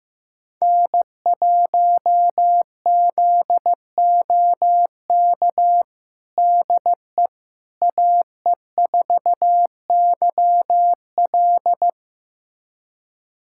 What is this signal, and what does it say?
Morse code, heard as a keyed tone, about 15 wpm, 700 Hz. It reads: N1ZOK DE AE4YL
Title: USA Callsign DE Callsign - 5 Characters - 15wpm